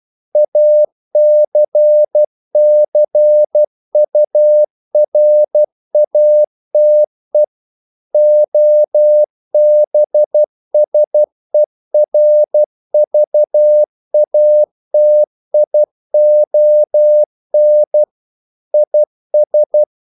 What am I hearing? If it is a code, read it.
Morse code, heard as a keyed tone, about 12 wpm, 600 Hz. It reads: ACCURATE OBSERVATION IS
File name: CW / Practice / EnglishQuotations0112